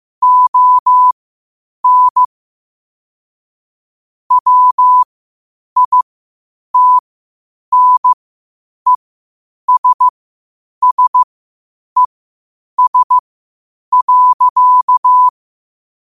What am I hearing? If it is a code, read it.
Morse code, heard as a keyed tone, about 15 wpm, spaced as for 8 wpm, 1000 Hz. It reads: ON WITNESSES.